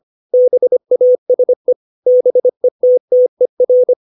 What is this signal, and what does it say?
Morse code, heard as a keyed tone, about 25 wpm, 500 Hz. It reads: BASE BETTER
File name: 632 words / 632 Words 25WPM